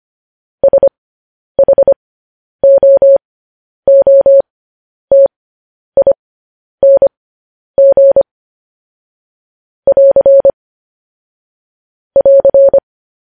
Morse code, heard as a keyed tone, about 25 wpm, spaced as for 10 wpm, 550 Hz. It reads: SHOOTING <AR> <AR>
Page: news headlines in morse code at 10 wpm